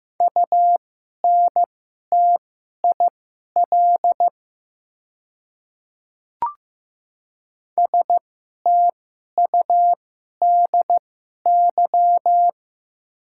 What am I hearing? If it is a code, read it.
Morse code, heard as a keyed tone, about 15 wpm, spaced as for 11 wpm, 690 Hz. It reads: UNTIL STUDY
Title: 2x Character Spacing - Top 500 Words - 15wpm